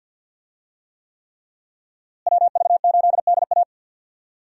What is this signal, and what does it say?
Morse code, heard as a keyed tone, about 50 wpm, 700 Hz. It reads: WV8DA
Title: USA Call Signs - 5 Characters - Repeated 3x - 50wpm